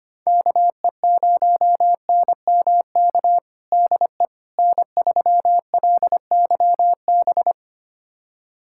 Morse code, heard as a keyed tone, about 25 wpm, 700 Hz. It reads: KE0NMK DE N3LYB
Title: USA Callsign DE Callsign 25wpm